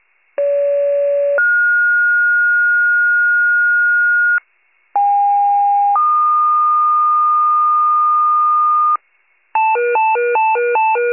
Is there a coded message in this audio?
Yes: two-tone page from 0.4 to 9.0 s, heard as a one-second tone, then a three-second tone, about 570 then 1430 Hz, about 810 then 1220 Hz, with noise.